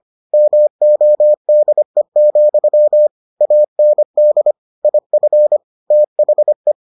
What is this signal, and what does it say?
Morse code, heard as a keyed tone, about 25 wpm, 600 Hz. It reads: MODE, AND IF THE